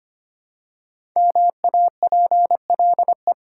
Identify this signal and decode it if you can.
Morse code, heard as a keyed tone, about 25 wpm, 700 Hz. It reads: MAPLE
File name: practice / 25wpm / Easy Words with Hints - 25wpm